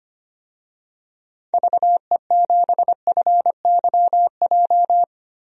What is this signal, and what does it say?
Morse code, heard as a keyed tone, about 25 wpm, 700 Hz. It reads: VE7FYJ
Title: DXCC-100 Call Signs - 6 Characters - 25wpm